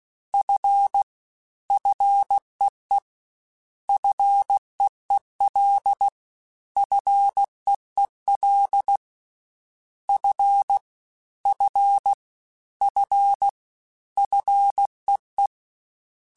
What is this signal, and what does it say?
Morse code, heard as a keyed tone, about 16 wpm, 800 Hz. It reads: F FEE FEEL FEEL F F F FEE
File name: Single F mp3s / Single F Prefixes 3x2 Twice @15